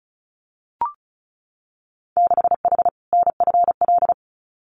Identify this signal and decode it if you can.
Morse code, heard as a keyed tone, about 36 wpm, 700 Hz. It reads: 6H NFL
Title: Winter Field Day Exchanges Encore 35wpm